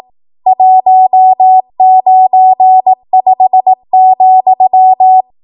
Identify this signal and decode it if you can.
Morse code, heard as a keyed tone, about 18 wpm, 750 Hz. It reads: 195,
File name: Morse / 18WPM